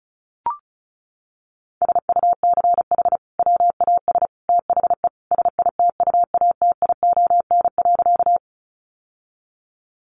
Morse code, heard as a keyed tone, about 34 wpm, 700 Hz. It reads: SUCH WAS THE SITUATION.
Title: Sentences from Top 1200 Words 35wpm